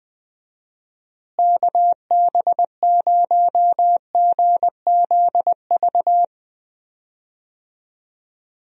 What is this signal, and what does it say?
Morse code, heard as a keyed tone, about 20 wpm, 700 Hz. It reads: KB0GZV
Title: USA Call Signs - 6 Characters - Repeated 3x - 20wpm